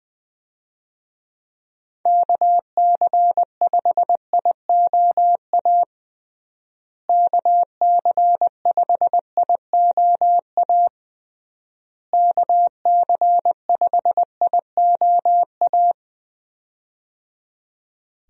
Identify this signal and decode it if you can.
Morse code, heard as a keyed tone, about 20 wpm, 700 Hz. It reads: KC5IOA KC5IOA KC5IOA
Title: USA Call Signs - 6 Characters - Repeated 3x - 20wpm